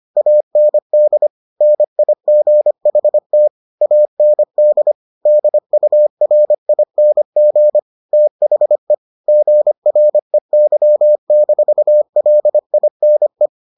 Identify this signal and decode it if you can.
Morse code, heard as a keyed tone, about 25 wpm, 600 Hz. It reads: AND NIGHT AND DURING THE GREY-LINE